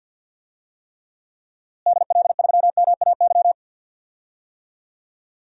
Morse code, heard as a keyed tone, about 48 wpm, 700 Hz. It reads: DL3MAY
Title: DXCC-100 Call Signs - 6 Characters - Repeated 3x - 50wpm